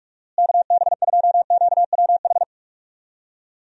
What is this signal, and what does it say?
Morse code, heard as a keyed tone, about 43 wpm, 700 Hz. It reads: KB2QWH
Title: USA Call Signs - 6 Characters - Repeated 3x - 45wpm